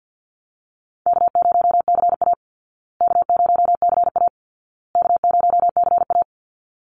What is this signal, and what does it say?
Morse code, heard as a keyed tone, about 51 wpm, 700 Hz. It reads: K0CA K0CA K0CA
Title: Call Signs Repeated 2x - 50wpm